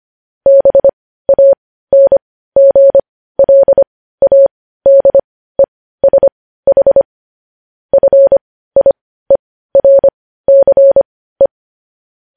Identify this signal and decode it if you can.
Morse code, heard as a keyed tone, about 25 wpm, spaced as for 15 wpm, 550 Hz. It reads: BANGLADESH FIERCE
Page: news headlines in morse code at 15 wpm